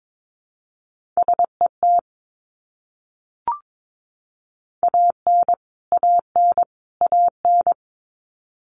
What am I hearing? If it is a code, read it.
Morse code, heard as a keyed tone, about 22 wpm, 700 Hz. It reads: SET AN AN AN
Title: Top 200 Words Repeated 22wpm